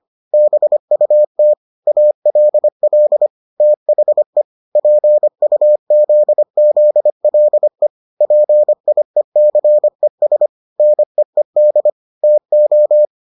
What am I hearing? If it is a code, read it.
Morse code, heard as a keyed tone, about 25 wpm, 600 Hz. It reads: BUT ALL THE PUZZLE PIECES NEED TO